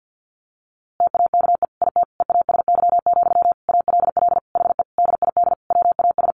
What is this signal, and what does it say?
Morse code, heard as a keyed tone, about 50 wpm, 700 Hz. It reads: TAKE IT EASY, ALL HE DID WAS
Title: A Sound Of Thunder 50wpm